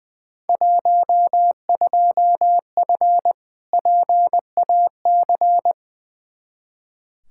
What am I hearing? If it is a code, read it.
Morse code, heard as a keyed tone, about 20 wpm, 700 Hz. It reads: 12F PAC
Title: Field Day Exchanges Encore - 20wpm